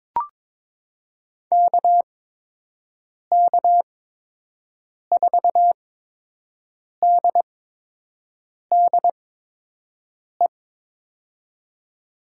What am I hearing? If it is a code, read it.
Morse code, heard as a keyed tone, about 22 wpm, spaced as for 6 wpm, 700 Hz. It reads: KK4DDE
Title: USA Call Signs - 8x Character Spacing - 22wpm